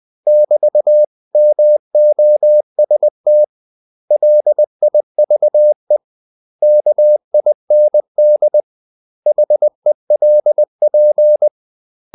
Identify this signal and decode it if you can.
Morse code, heard as a keyed tone, about 20 wpm, 600 Hz. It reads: <BT>MOST LIVE KIND HELP
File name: CW / Practice / 500MostCommonEnglishWords20